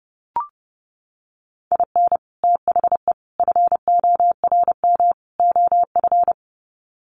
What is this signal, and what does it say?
Morse code, heard as a keyed tone, about 30 wpm, 700 Hz. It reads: IN THE FORM OF
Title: Sets of 4 Words 30wpm